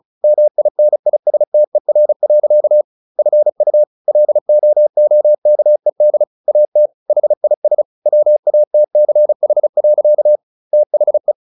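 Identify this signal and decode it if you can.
Morse code, heard as a keyed tone, about 34 wpm, 600 Hz. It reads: MINISTER. FU LOOKED AT HIS WATCH. THE
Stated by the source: Morse practice at 35 wpm from CW med Gnister, sending Ken Follett's novel Never in English.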